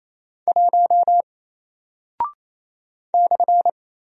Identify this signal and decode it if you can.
Morse code, heard as a keyed tone, about 28 wpm, 700 Hz. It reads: 1 /
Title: Single Character (Letter:Number:Punctuation:Prosign) Alt 28wpm